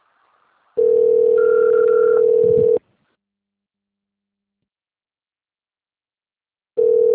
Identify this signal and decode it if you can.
Ringback tone (North America), 440 + 480 Hz.
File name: out- - - - - .31470